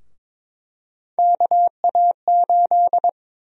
Morse code, heard as a keyed tone, about 22 wpm, 700 Hz. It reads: KA8
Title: USA Call Sign Prefixes - 22wpm